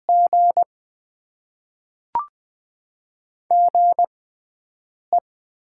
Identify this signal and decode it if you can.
Morse code, heard as a keyed tone, about 20 wpm, spaced as for 7 wpm, 700 Hz. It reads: G GE
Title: Top 90 QSO Elements - 6x Character Spacing - 20wpm